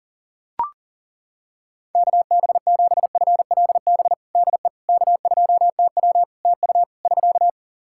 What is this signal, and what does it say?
Morse code, heard as a keyed tone, about 37 wpm, 700 Hz. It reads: KB7FLB DE K2TW TU <SK>
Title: Practice QSO - Exchange 4 - Ending - 40wpm